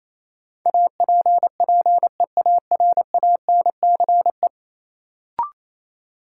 Morse code, heard as a keyed tone, about 28 wpm, 700 Hz. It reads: APPEARANCE
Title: Sentences from Top 1100 Words - New Word Drill Alt - 28wpm